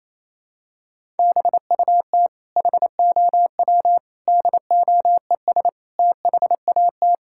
Morse code, heard as a keyed tone, about 28 wpm, 700 Hz. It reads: BUT HOW DOES THAT
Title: Sentences from Top 1000 Words - Review Part 6 - 28wpm